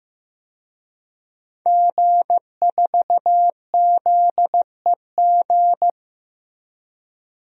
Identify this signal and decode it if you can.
Morse code, heard as a keyed tone, about 16 wpm, 700 Hz. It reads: G4ZEG
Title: DXCC-100 Call Signs - 5 Characters - Repeated 3x - 15wpm